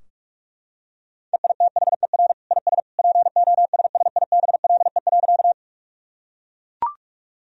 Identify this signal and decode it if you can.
Morse code, heard as a keyed tone, about 42 wpm, 700 Hz. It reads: EITHER IS POSSIBLE.
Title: Sentences from Top 1000 Words - Review Part 4 - 45wpm